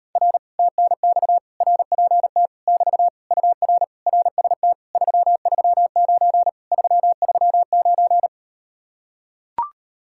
Morse code, heard as a keyed tone, about 39 wpm, 700 Hz. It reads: R TNX RPT <BT> UR RST 339 339